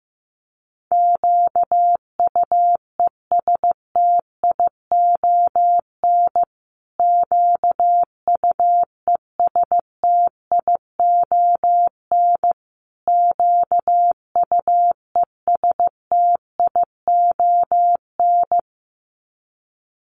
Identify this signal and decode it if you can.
Morse code, heard as a keyed tone, about 15 wpm, 700 Hz. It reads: QUESTION QUESTION QUESTION
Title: Top 500 Words Repeated 15wpm